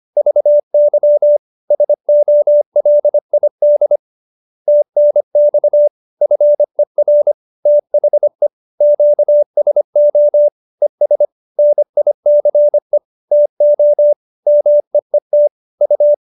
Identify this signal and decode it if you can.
Morse code, heard as a keyed tone, about 25 wpm, 590 Hz. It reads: VY SOLID TNX FER THE QSO ES NICE TO MEET U